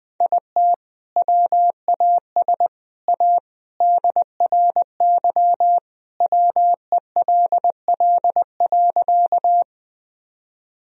Morse code, heard as a keyed tone, about 20 wpm, 700 Hz. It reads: IT WAS A DRY WELL.
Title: Sentences from Top 1000 Words - Review Part 3 - 20wpm